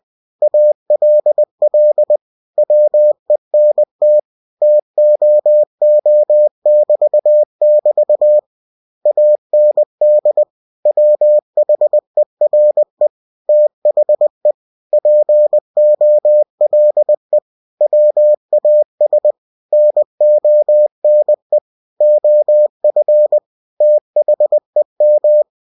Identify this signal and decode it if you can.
Morse code, heard as a keyed tone, about 20 wpm, 600 Hz. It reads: ALL WENT TOO<BT><BT> AND WHERE THE POLE WAS NONE OF THEM
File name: cw-books / winnie-the-pooh / winnie-20wpm-page-0081